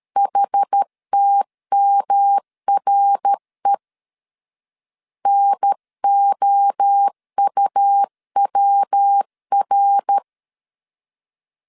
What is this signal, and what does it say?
Morse code, heard as a keyed tone, about 13 wpm, 800 Hz. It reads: HTMRE NOUWR
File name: CW / DET / 1 3 1500C S A 12wpm 2905